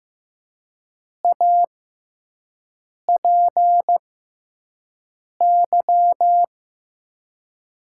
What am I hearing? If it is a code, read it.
Morse code, heard as a keyed tone, about 15 wpm, spaced as for 5 wpm, 700 Hz. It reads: APY
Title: USA Call Signs - 6x Character Spacing - 15wpm